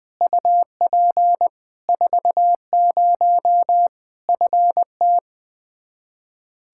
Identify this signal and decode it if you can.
Morse code, heard as a keyed tone, about 20 wpm, 700 Hz. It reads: UP 40 FT
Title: Practice QSO - Exchange 2 - Info - 20wpm